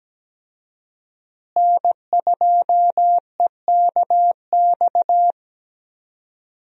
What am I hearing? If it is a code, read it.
Morse code, heard as a keyed tone, about 17 wpm, 700 Hz. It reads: N2EKX
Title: USA Call Signs - 5 Characters - Repeated 2x - 17wpm